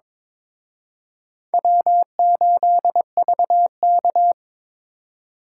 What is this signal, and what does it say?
Morse code, heard as a keyed tone, about 26 wpm, 700 Hz. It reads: W8VK
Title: Speed Racing - USA 1x2 Call Signs 22wpm